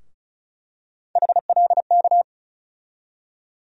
Morse code, heard as a keyed tone, about 36 wpm, 700 Hz. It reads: HLK
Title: USA Call Sign Postfixes - 3 Letters 35wpm